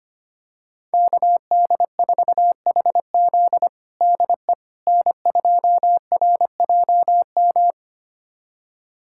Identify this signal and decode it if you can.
Morse code, heard as a keyed tone, about 25 wpm, 700 Hz. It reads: KD4HZ DE N2RJM
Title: USA Callsign DE Callsign - 5 Characters - 25wpm